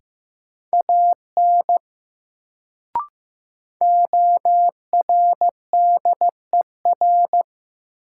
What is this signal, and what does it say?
Morse code, heard as a keyed tone, about 15 wpm, 700 Hz. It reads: AN ORDER
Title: Sets of 2 Words - New Word Drill - 15wpm